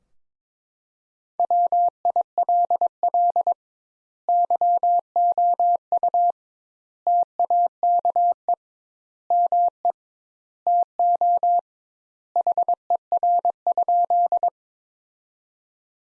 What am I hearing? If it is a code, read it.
Morse code, heard as a keyed tone, about 22 wpm, 700 Hz. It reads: WILL YOU TAKE ME TO HER?
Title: Sentences Top 200 Words - 22wpm